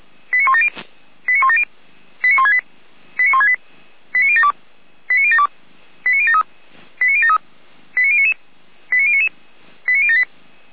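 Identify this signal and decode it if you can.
Five-tone selective call (ZVEI), heard as quick runs of five tones, tones of about 70 ms. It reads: 89170 89170 89168 89168 89072 89072 89073 89073 89099 89099 89078
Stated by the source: ZVEI five-tone selective call received on SDR.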